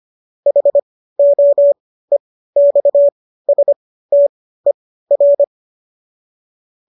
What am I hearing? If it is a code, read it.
Morse code, heard as a keyed tone, about 25 wpm, spaced as for 15 wpm, 570 Hz. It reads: HOEXSTER
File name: qso219 15